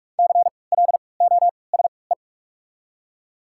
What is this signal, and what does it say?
Morse code, heard as a keyed tone, about 42 wpm, spaced as for 25 wpm, 700 Hz. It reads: CLOSE
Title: 3x Character Spacing - Top 300 Words - 45wpm